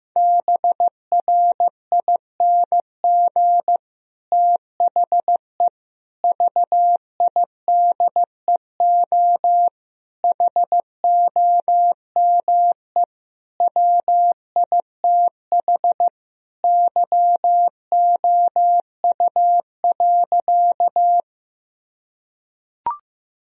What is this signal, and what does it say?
Morse code, heard as a keyed tone, about 15 wpm, 700 Hz. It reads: BRING THE VIDEO HOME WITH YOU.
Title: Sentences from Top 1000 Words - Review Part 4 - 15wpm